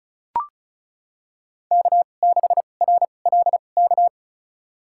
Morse code, heard as a keyed tone, about 36 wpm, spaced as for 25 wpm, 700 Hz. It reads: K6RLK
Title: USA Call Signs - 5 Characters - 2x Character Spacing - 35wpm